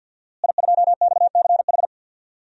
Morse code, heard as a keyed tone, about 49 wpm, 700 Hz. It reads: I2XCH
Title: DXCC-100 Call Signs - 5 Characters - Repeated 3x - 50wpm